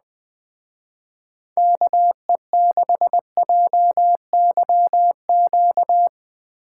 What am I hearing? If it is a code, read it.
Morse code, heard as a keyed tone, about 20 wpm, 700 Hz. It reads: KE6JYQ